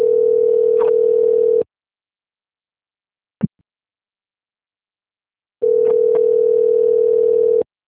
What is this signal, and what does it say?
Ringback tone (North America), 440 + 480 Hz.